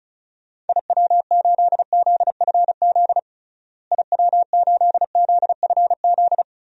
Morse code, heard as a keyed tone, about 36 wpm, 700 Hz. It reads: IW8ZFZ IW8ZFZ
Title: DXCC-100 Call Signs - 6 Characters - Repeated 3x - 35wpm